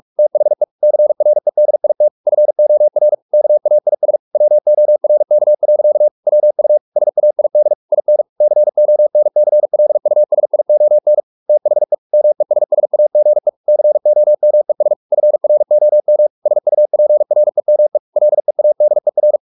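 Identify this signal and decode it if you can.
Morse code, heard as a keyed tone, about 44 wpm, 600 Hz. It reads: THE CREDIT FOR KAIS WORK. WU SAID IN CONCLUSION THE MESSAGE COMES FROM SUPREME LEADER
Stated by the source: Morse practice at 45 wpm, the novel Never by Ken Follett in English, from CW med Gnister.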